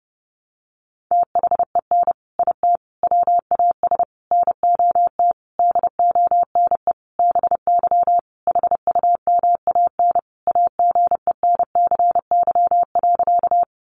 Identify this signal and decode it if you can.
Morse code, heard as a keyed tone, about 30 wpm, 700 Hz. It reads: THEN IT WAS NOT DONE BY HUMAN AGENCY.
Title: Sentences from Top 1300 Words 30wpm